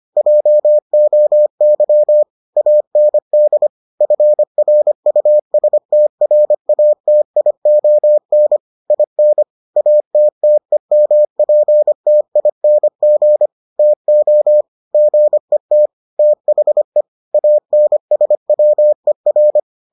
Morse code, heard as a keyed tone, about 25 wpm, 600 Hz. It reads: JOY AND FRUSTRATION IN ATTEMPTING TO GET THE ANSWER